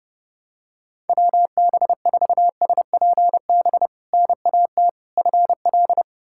Morse code, heard as a keyed tone, about 32 wpm, 700 Hz. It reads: WB4SPB NAT FL
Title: K1USN SST QSO Practice - 30wpm